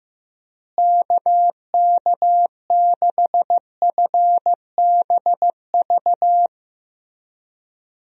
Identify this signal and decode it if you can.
Morse code, heard as a keyed tone, about 15 wpm, 700 Hz. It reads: KK6FBV
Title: USA Call Signs - 6 Characters - Repeated 3x - 15wpm